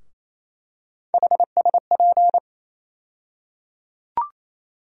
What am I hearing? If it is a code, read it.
Morse code, heard as a keyed tone, about 27 wpm, 700 Hz. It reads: HSP
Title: USA Call Sign Postfixes - 3 Letters 28wpm